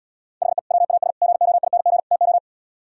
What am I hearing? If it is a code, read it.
Morse code, heard as a keyed tone, about 77 wpm, 700 Hz. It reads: HE WAS GOING TO